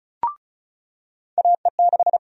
Morse code, heard as a keyed tone, about 36 wpm, 700 Hz. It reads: AE6
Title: USA Call Sign Prefixes - 35wpm